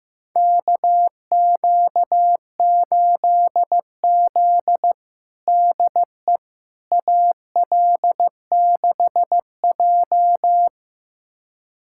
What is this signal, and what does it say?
Morse code, heard as a keyed tone, about 16 wpm, 700 Hz. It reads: KQ8Z DE AL6J
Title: USA Callsign DE Callsign - 4 Characters - 15wpm